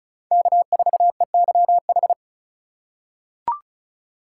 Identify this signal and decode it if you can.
Morse code, heard as a keyed tone, about 36 wpm, 700 Hz. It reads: K4EYH
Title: USA Call Signs - 5 Characters - 4x Character Spacing - 35wpm